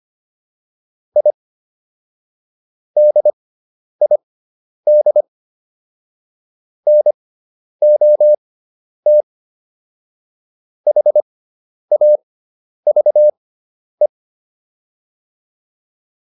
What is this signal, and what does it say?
Morse code, heard as a keyed tone, about 25 wpm, spaced as for 10 wpm, 600 Hz. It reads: I DID NOT HAVE
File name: WD304 10